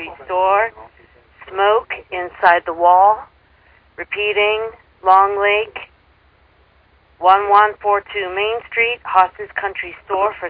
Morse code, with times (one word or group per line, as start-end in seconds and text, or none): none